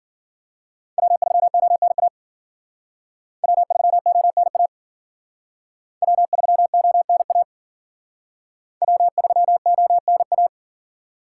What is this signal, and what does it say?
Morse code, heard as a keyed tone, about 52 wpm, 700 Hz. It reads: W3ONA W3ONA W3ONA W3ONA